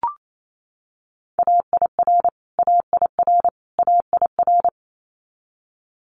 Morse code, heard as a keyed tone, about 28 wpm, 700 Hz. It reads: AIR AIR AIR
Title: Multi-letter Phonemes Repeated 3x 28wpm